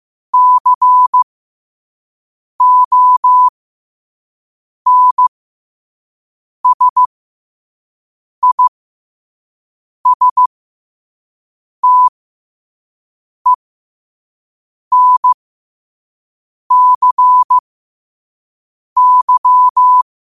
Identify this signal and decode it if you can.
Morse code, heard as a keyed tone, about 15 wpm, spaced as for 5 wpm, 1000 Hz. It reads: CONSISTENCY
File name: qotdfiles / 5wpm